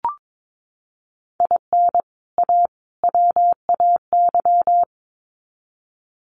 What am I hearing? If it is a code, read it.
Morse code, heard as a keyed tone, about 22 wpm, 700 Hz. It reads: IN A WAY
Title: Sets of 3 Words 22wpm